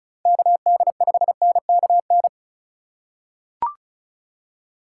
Morse code, heard as a keyed tone, about 34 wpm, 700 Hz. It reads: KD5NKN
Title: USA Call Signs - 6 Characters - 3x Character Spacing - 35wpm